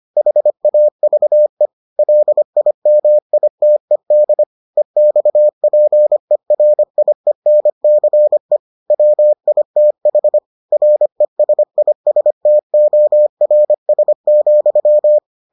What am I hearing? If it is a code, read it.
Morse code, heard as a keyed tone, about 25 wpm, 600 Hz. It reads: HAVE LIMITED EXPERIENCE WITH RESISTORS,